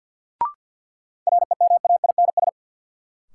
Morse code, heard as a keyed tone, about 46 wpm, 700 Hz. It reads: REMAINS
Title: Sets of 5 Words Encore - New Word Drill - 50wpm